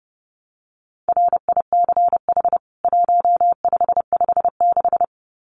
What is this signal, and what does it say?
Morse code, heard as a keyed tone, about 30 wpm, 700 Hz. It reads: RICH 1556
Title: CWT Exchange 30wpm